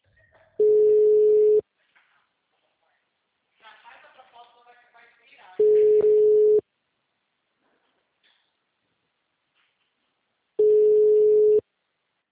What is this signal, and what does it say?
Ringback tone (Europe, 425 Hz), with speech.
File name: out- -2053- - - .94343